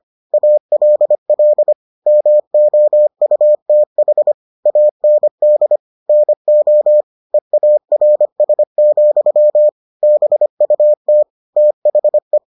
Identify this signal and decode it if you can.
Morse code, heard as a keyed tone, about 25 wpm, 600 Hz. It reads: ALL MOUTH AND NO EARS, BUT THE